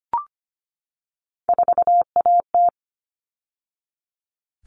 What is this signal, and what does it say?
Morse code, heard as a keyed tone, about 25 wpm, 700 Hz. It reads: 4AT
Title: All Cut Numbers - 3-Digit 25wpm